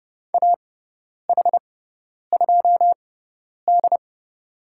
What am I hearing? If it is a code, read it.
Morse code, heard as a keyed tone, about 30 wpm, spaced as for 10 wpm, 700 Hz. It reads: AH2D
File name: Lesson 16g - Letter c - Call Signs 30wpm